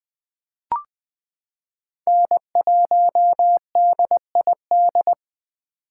Morse code, heard as a keyed tone, about 21 wpm, 700 Hz. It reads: N1DID
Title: K1USN SST QSO Practice - 20wpm